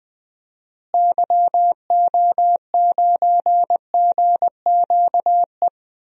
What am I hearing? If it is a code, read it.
Morse code, heard as a keyed tone, about 21 wpm, 700 Hz. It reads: YO9GQE
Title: DXCC-100 Call Signs - 6 Characters - Repeated 2x - 20wpm